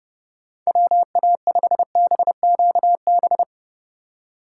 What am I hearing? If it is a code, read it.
Morse code, heard as a keyed tone, about 32 wpm, 700 Hz. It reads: WA5BQB
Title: USA Call Signs - 6 Characters - 4x Character Spacing - 30wpm